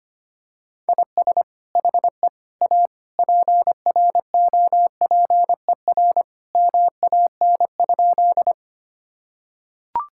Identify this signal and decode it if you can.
Morse code, heard as a keyed tone, about 25 wpm, 700 Hz. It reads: IS HE A PROPER MAN?
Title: Sentences from Top 1000 Words - Review Part 5 - 25wpm